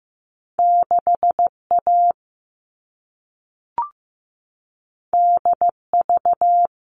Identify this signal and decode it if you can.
Morse code, heard as a keyed tone, about 15 wpm, 700 Hz. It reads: 6A DV
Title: All Cut Numbers - 2-Digit 15wpm